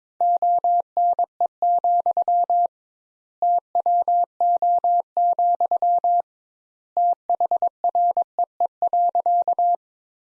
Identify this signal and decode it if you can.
Morse code, heard as a keyed tone, about 22 wpm, 700 Hz. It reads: ONE, TWO, THREE.
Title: Sentences Top 200 Words - 22wpm